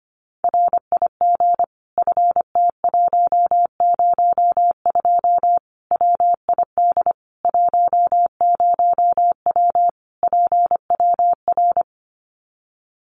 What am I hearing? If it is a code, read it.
Morse code, heard as a keyed tone, about 25 wpm, 700 Hz. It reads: RIG FT102 WID 10W PWR
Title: Common QSO Phrases 25wpm